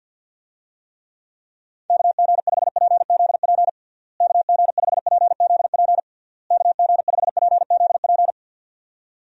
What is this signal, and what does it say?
Morse code, heard as a keyed tone, about 51 wpm, 700 Hz. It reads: KG5PZP KG5PZP KG5PZP
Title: USA Call Signs - 6 Characters - Repeated 3x - 50wpm